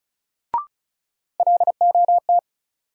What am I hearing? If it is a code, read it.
Morse code, heard as a keyed tone, about 36 wpm, 700 Hz. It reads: LOT
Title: Sets of 2 Words - New Word Drill - 35wpm